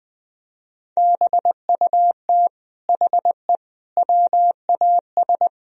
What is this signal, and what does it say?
Morse code, heard as a keyed tone, about 20 wpm, 700 Hz. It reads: BUT HE WAS
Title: Sentences from Top 1000 Words - Review Part 3 - 20wpm